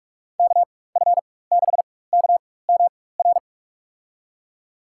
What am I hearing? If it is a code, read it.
Morse code, heard as a keyed tone, about 45 wpm, spaced as for 21 wpm, 700 Hz. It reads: KF6KMR
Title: USA Call Signs - 6 Characters - 4x Character Spacing - 45wpm